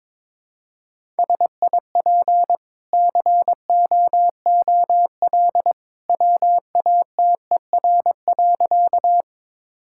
Morse code, heard as a keyed tone, about 22 wpm, 700 Hz. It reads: SIP COOL WATER.
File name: practice / 22wpm / Easy Sentences with Hints - 22wpm